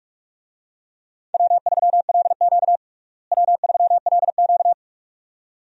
Morse code, heard as a keyed tone, about 42 wpm, 700 Hz. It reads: W3LQ W3LQ
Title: USA Call Signs - 4 Characters - Repeated 2x - 45wpm